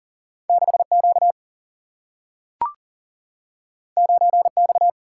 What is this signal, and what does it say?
Morse code, heard as a keyed tone, about 40 wpm, 700 Hz. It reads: 6Q 9X
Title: Single-Letter Single-Number 40wpm